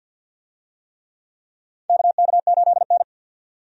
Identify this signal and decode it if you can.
Morse code, heard as a keyed tone, about 53 wpm, 700 Hz. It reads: KK8N
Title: USA Call Signs - 4 Characters - 50wpm